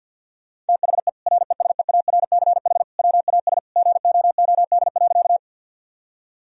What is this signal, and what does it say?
Morse code, heard as a keyed tone, about 50 wpm, 700 Hz. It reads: THE RESEARCH WAS GOOD.
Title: Sentences from Top 1000 Words - Review Part 4 - 50wpm